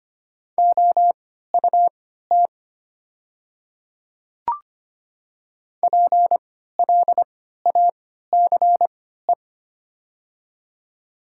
Morse code, heard as a keyed tone, about 25 wpm, spaced as for 14 wpm, 700 Hz. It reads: OUT PLACE
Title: 3x Character Spacing - Top 200 Words - 25wpm